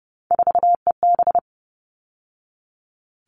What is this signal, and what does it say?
Morse code, heard as a keyed tone, about 30 wpm, 700 Hz. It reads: 4EB
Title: All Cut Numbers - 3-Digit 30wpm